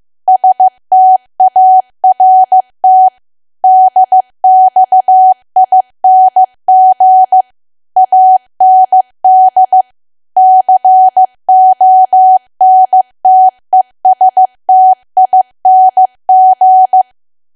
Morse code, heard as a keyed tone, about 15 wpm, 750 Hz. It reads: START DXING AND CONTESTING
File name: cwa / W1AW-Level2 / 15WPM